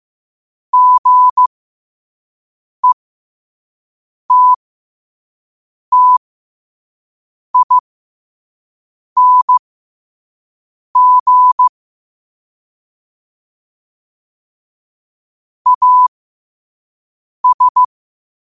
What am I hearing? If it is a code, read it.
Morse code, heard as a keyed tone, about 15 wpm, spaced as for 5 wpm, 1000 Hz. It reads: GETTING AS
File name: qotdfiles / 5wpm